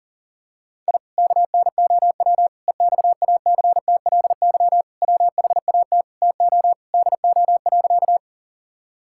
Morse code, heard as a keyed tone, about 40 wpm, 700 Hz. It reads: I KNOW EXACTLY WHAT TO DO.
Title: Sentences from Top 1000 Words - Review Part 6 - 40wpm